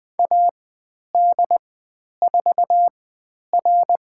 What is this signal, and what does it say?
Morse code, heard as a keyed tone, about 20 wpm, spaced as for 10 wpm, 700 Hz. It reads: AD4R
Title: 102 Lesson 17g - Letter u - Call Signs 20wpm